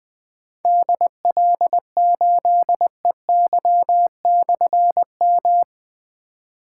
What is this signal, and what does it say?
Morse code, heard as a keyed tone, about 20 wpm, 700 Hz. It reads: DL8EY/M
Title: Difficult Call Signs 20wpm